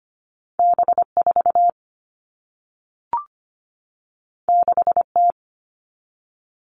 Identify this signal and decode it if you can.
Morse code, heard as a keyed tone, about 25 wpm, 700 Hz. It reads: B4 6T
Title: All Cut Numbers - 2-Digit 25wpm